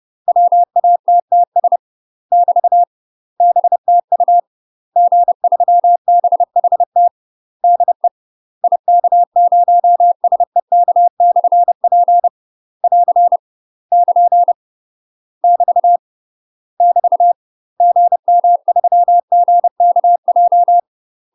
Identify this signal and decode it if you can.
Morse code, heard as a keyed tone, about 30 wpm, 700 Hz. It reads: WATTS <BT> BTU G3BHT DE IK0SEK/P <AR> <KN> <BT> <BT> GM3GKJ
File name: UK MASTER QSOs TRAINING / 30WPM / UK MASTER QSOs 0021